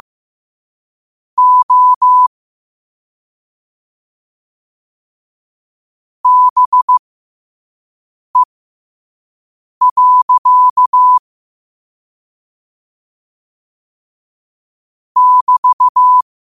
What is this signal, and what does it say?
Morse code, heard as a keyed tone, about 15 wpm, spaced as for 5 wpm, 1000 Hz. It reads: O BE. <BT>
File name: 2024-10-07-5wpm